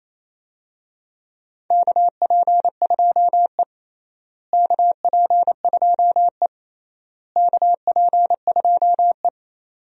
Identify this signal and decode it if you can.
Morse code, heard as a keyed tone, about 28 wpm, 700 Hz. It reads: KP2E KP2E KP2E
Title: DXCC-100 Call Signs - 4 Characters - Repeated 3x - 28wpm